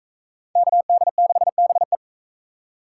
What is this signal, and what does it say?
Morse code, heard as a keyed tone, about 44 wpm, 700 Hz. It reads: KD6BE